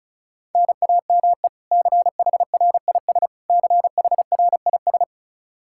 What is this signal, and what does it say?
Morse code, heard as a keyed tone, about 36 wpm, 700 Hz. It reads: NAME CHRIS CHRIS
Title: Practice QSO - Exchange 1 - Beginning - 35wpm